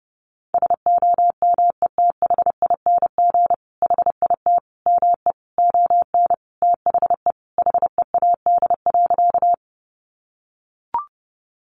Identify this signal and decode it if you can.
Morse code, heard as a keyed tone, about 28 wpm, 700 Hz. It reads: SOMETHING HIT ME ON THE HEAD.
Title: Sentences from Top 1200 Words 30wpm